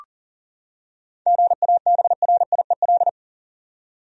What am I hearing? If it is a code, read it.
Morse code, heard as a keyed tone, about 37 wpm, 700 Hz. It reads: GABRIEL
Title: USA Common Names - 40wpm